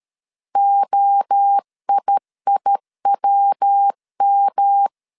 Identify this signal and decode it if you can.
Morse code, heard as a keyed tone, about 13 wpm, 800 Hz. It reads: OIIWM
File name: CW / DET / 1 3 1500C S C 12wpm 2750